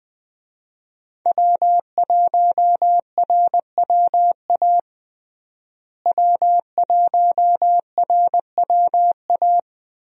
Morse code, heard as a keyed tone, about 20 wpm, 700 Hz. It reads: W1RWA W1RWA
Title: USA Call Signs - 5 Characters - Repeated 2x - 20wpm